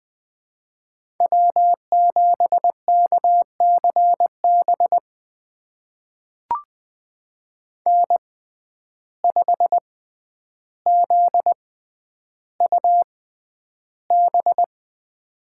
Morse code, heard as a keyed tone, about 21 wpm, spaced as for 10 wpm, 700 Hz. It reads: W7KCB N5ZUB
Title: USA Call Signs - 5 Characters - 6x Character Spacing - 20wpm